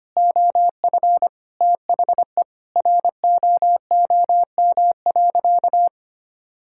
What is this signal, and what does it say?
Morse code, heard as a keyed tone, about 25 wpm, 700 Hz. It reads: OF THE ROOM.
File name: practice / 25wpm / Sentences with 4-Word Phrases - 25wpm